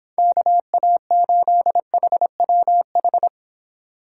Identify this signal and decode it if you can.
Morse code, heard as a keyed tone, about 26 wpm, 700 Hz. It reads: KA8HWH